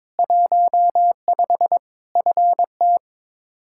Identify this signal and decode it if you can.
Morse code, heard as a keyed tone, about 22 wpm, 700 Hz. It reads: 15 FT
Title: K1USN SST QSO Practice 22wpm